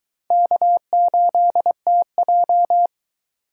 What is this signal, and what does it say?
Morse code, heard as a keyed tone, about 22 wpm, 700 Hz. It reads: K8TJ